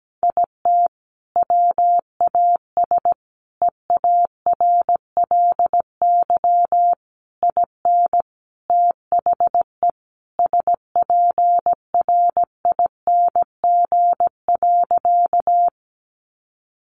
Morse code, heard as a keyed tone, about 17 wpm, 700 Hz. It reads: IT WAS EARLY IN THE SPRING.
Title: Sentences from Top 1200 Words 17wpm